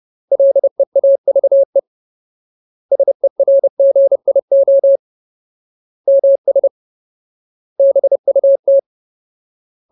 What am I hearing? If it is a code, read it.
Morse code, heard as a keyed tone, about 30 wpm, 550 Hz. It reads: LEAVE SERGIO MS BUT
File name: Ultimate Word List / 30WPM Ultimate